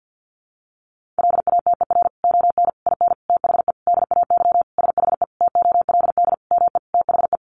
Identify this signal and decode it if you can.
Morse code, heard as a keyed tone, about 48 wpm, 700 Hz. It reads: LATER ON IN THE DAY SHE TOLD ME THE
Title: Sentences from Top 700 Words 50wpm